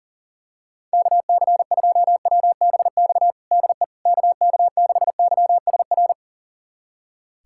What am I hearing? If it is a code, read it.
Morse code, heard as a keyed tone, about 40 wpm, 700 Hz. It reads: KC2WBX DE KK6YSR
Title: USA Callsign DE Callsign - 6 Characters - 40wpm